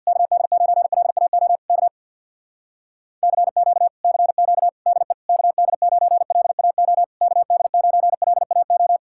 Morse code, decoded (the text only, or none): KD9LAO K CQ CQ DE KD9LAO KD9LAO